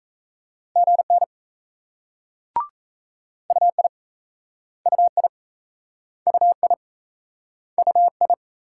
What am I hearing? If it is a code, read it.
Morse code, heard as a keyed tone, about 38 wpm, 700 Hz. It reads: GN UI UI UI UI